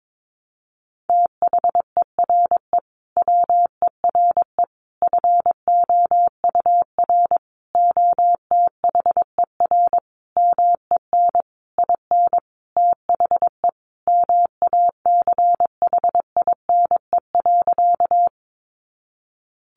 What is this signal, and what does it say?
Morse code, heard as a keyed tone, about 22 wpm, 700 Hz. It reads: THERE WERE FOUR OTHER MEN IN THE MACHINE.
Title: A Sound Of Thunder 22wpm